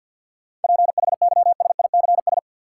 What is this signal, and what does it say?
Morse code, heard as a keyed tone, about 51 wpm, 700 Hz. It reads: PHYSICS
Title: Sentences from Top 1500 Words - New Word Drill Alt - 50wpm